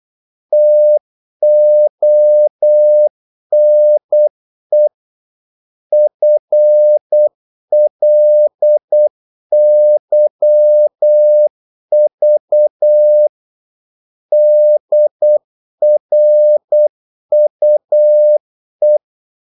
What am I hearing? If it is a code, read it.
Morse code, heard as a keyed tone, about 8 wpm, 600 Hz. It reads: TONE FLYV DRUE
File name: Korte ord DK 8wpm